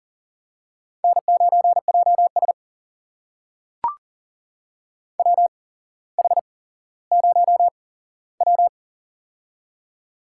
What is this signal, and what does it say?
Morse code, heard as a keyed tone, about 38 wpm, spaced as for 17 wpm, 700 Hz. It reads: N9JS WH0W
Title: USA Call Signs - 4 Characters - 8x Character Spacing - 40wpm